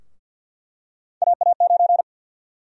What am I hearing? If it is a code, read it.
Morse code, heard as a keyed tone, about 46 wpm, 700 Hz. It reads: AA9
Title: USA Call Sign Prefixes - 50wpm